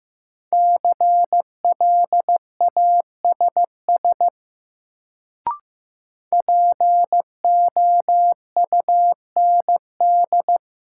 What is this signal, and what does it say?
Morse code, heard as a keyed tone, about 15 wpm, 700 Hz. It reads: CLASS POUND
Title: Sentences from Top 1000 Words - Review Part 2 - New Word Drill - 15wpm